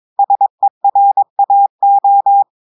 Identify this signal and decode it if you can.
Morse code, heard as a keyed tone, about 22 wpm, 800 Hz. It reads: SERAO
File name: audios / portugais / portugais texte en clair 2 vitesse 22 mots minutes